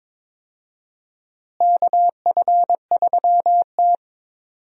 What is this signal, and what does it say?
Morse code, heard as a keyed tone, about 22 wpm, 700 Hz. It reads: KF3T
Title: USA Call Signs - 4 Characters - Repeated 3x - 22wpm